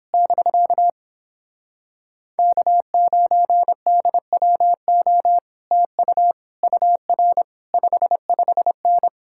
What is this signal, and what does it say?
Morse code, heard as a keyed tone, about 29 wpm, 700 Hz. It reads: <BK> K9DWO TU UR 55N